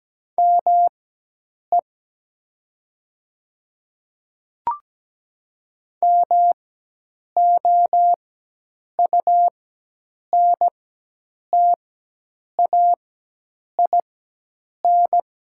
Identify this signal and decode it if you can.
Morse code, heard as a keyed tone, about 17 wpm, spaced as for 8 wpm, 700 Hz. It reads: ME MOUNTAIN
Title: 4x Character Spacing - Top 500 Words - 17wpm